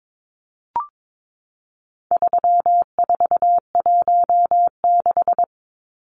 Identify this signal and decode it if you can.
Morse code, heard as a keyed tone, about 22 wpm, 700 Hz. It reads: 3416
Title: Numbers - 4-Digit 22wpm